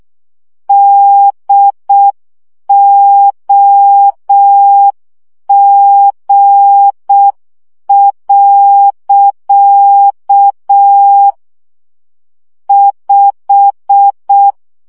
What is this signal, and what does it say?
Morse code, heard as a keyed tone, about 6 wpm, 800 Hz. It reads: DOG. 5
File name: CW 5 baud